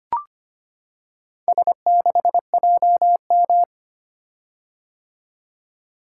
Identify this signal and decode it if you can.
Morse code, heard as a keyed tone, about 25 wpm, 700 Hz. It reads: S6JM
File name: practice / 25wpm / Four Letter-Number Combinations - 25wpm